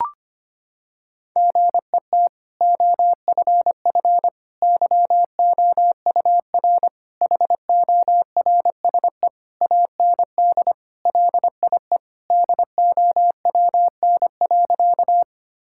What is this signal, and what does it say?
Morse code, heard as a keyed tone, about 25 wpm, 700 Hz. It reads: GET OFF YOUR HORSE AND LIE DOWN.
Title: Sentences from Top 1600 Words 25wpm